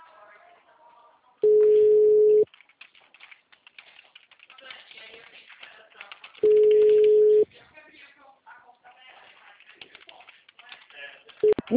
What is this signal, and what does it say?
Ringback tone (Europe, 425 Hz), with speech.